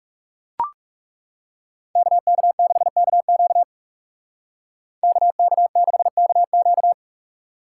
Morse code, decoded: KK6KQ KK6KQ

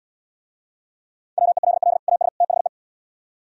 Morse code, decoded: WELL AS THE

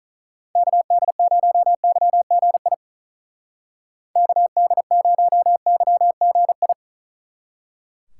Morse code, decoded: KD0YGI KD0YGI

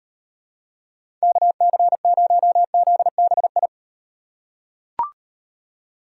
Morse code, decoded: KC0ZBI

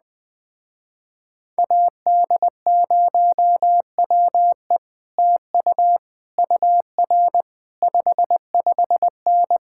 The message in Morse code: AD0WE TU UR 55N

